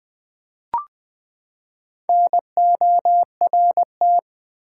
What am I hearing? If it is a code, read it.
Morse code, heard as a keyed tone, about 20 wpm, 700 Hz. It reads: NORT
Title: Build A Word - 4 Letters At A Time - 8-Letter Words - 20wpm